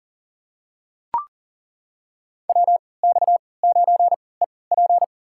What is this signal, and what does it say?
Morse code, heard as a keyed tone, about 40 wpm, spaced as for 23 wpm, 700 Hz. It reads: WX9EP
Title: USA Call Signs - 5 Characters - 3x Character Spacing - 40wpm